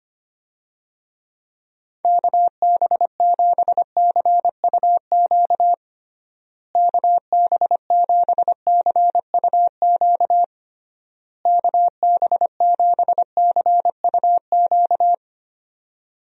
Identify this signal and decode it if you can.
Morse code, heard as a keyed tone, about 25 wpm, 700 Hz. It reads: KB7CUQ KB7CUQ KB7CUQ
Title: USA Call Signs - 6 Characters - Repeated 3x - 25wpm